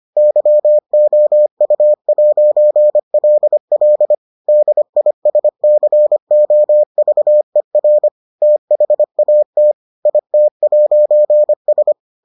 Morse code, heard as a keyed tone, about 25 wpm, 600 Hz. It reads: YOU'LL DISCOVER THAT IT'S